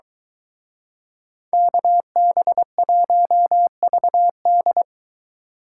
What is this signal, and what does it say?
Morse code, heard as a keyed tone, about 22 wpm, 700 Hz. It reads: KB1VD